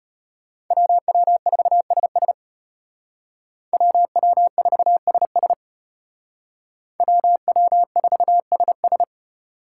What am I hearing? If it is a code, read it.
Morse code, heard as a keyed tone, about 30 wpm, 700 Hz. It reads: WW4SS WW4SS WW4SS
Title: Speed Racing - USA 2x2 Call Signs 25wpm